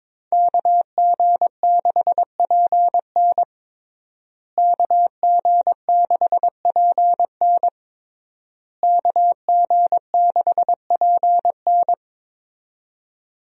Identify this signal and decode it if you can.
Morse code, heard as a keyed tone, about 21 wpm, 700 Hz. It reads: KG6PN KG6PN KG6PN
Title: USA Call Signs - 5 Characters - Repeated 3x - 22wpm